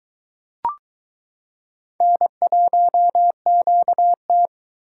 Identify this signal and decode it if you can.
Morse code, heard as a keyed tone, about 22 wpm, 700 Hz. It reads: N1QT